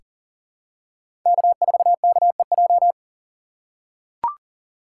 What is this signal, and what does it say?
Morse code, heard as a keyed tone, about 37 wpm, 700 Hz. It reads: K4KEJ
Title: USA Call Signs - 5 Characters - 4x Character Spacing - 40wpm